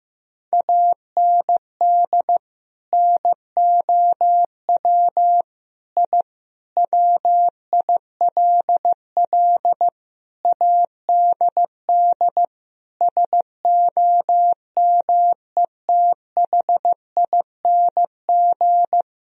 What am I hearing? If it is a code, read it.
Morse code, heard as a keyed tone, about 15 wpm, 700 Hz. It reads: AND NOW I WILL ADD SOMETHING
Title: Sentences from Top 1000 Words - Review Part 1 - 15wpm